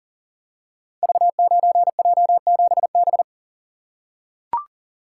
Morse code, heard as a keyed tone, about 38 wpm, 700 Hz. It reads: V9J7B
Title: Code Groups - 5 Letter:Numbers - 40wpm